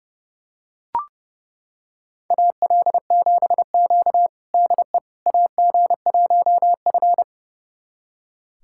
Morse code, heard as a keyed tone, about 31 wpm, 700 Hz. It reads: AL7Q DE AG1F
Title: USA Callsign DE Callsign - 4 Characters - 30wpm